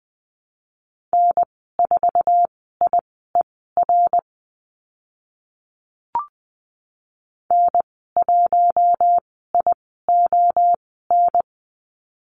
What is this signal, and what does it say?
Morse code, heard as a keyed tone, about 20 wpm, spaced as for 15 wpm, 700 Hz. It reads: N4IER N1ION
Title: 082 Lesson 14g - Number 2 - Call Signs 20wpm